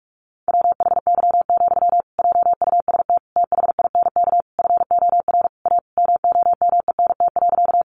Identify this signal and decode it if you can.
Morse code, heard as a keyed tone, about 44 wpm, 700 Hz. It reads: WHY, JUST THINK FOR A MOMENT.
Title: Sentences from Top 1000 Words 45wpm